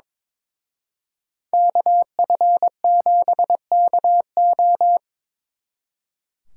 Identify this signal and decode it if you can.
Morse code, heard as a keyed tone, about 25 wpm, 700 Hz. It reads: KF7KO
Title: Speed Racing - USA 2x2 Call Signs 22wpm